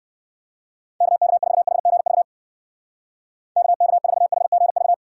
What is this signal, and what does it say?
Morse code, heard as a keyed tone, about 56 wpm, 700 Hz. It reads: KK4HGV KK4HGV
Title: Speed Racing - USA 2x3 Call Signs 45wpm